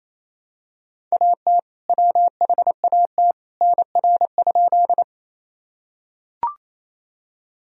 Morse code, heard as a keyed tone, about 28 wpm, 700 Hz. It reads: AT WHAT NR?
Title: Sentences from Top 100 Words and Abbreviations 28wpm